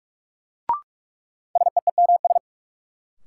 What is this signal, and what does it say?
Morse code, heard as a keyed tone, about 44 wpm, 700 Hz. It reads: SEEMS
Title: Sets of 3 Words - New Word Drill - 45wpm